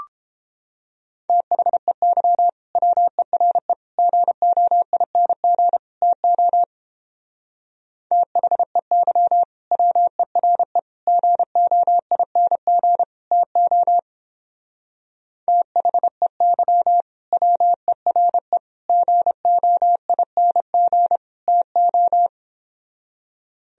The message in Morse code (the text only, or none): THEY WERE GOING TO THEY WERE GOING TO THEY WERE GOING TO